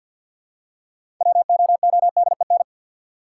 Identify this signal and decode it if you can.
Morse code, heard as a keyed tone, about 52 wpm, 700 Hz. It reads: WOODEN
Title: Six Letter Words - 50wpm